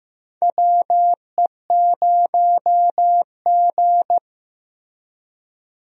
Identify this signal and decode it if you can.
Morse code, heard as a keyed tone, about 15 wpm, 700 Hz. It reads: WE0G
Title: USA Call Signs - 4 Characters - 15wpm